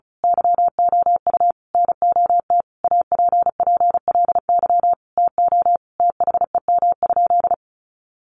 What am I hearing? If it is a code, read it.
Morse code, heard as a keyed tone, about 34 wpm, 700 Hz. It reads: YOU NOT APPLY TO THEM?
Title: Sentences from Top 1200 Words 35wpm